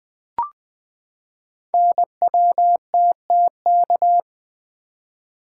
Morse code, heard as a keyed tone, about 20 wpm, 700 Hz. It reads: NWTTK
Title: Code Groups - 5 Letters - 20wpm